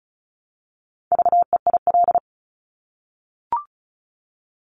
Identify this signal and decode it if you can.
Morse code, heard as a keyed tone, about 36 wpm, 700 Hz. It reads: VEIL
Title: Multi-letter Phonemes Repeated 3x 35wpm